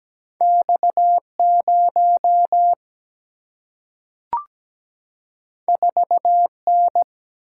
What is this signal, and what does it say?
Morse code, heard as a keyed tone, about 17 wpm, 700 Hz. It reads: X0 4N
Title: Single-Letter Single-Number 17wpm